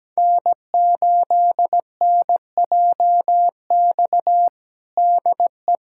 Morse code, heard as a keyed tone, about 17 wpm, 700 Hz. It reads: N8NJX DE
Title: USA Callsign DE Callsign - 5 Characters - 17wpm